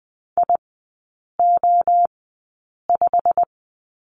Morse code, heard as a keyed tone, about 20 wpm, spaced as for 8 wpm, 700 Hz. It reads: IO5
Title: 121 Lesson 20e - Number 3 - Three Characters 20wpm